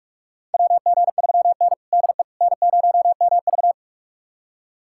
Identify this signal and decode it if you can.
Morse code, heard as a keyed tone, about 46 wpm, 700 Hz. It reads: WG3N DE N0MV
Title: USA Callsign DE Callsign - 4 Characters - 45wpm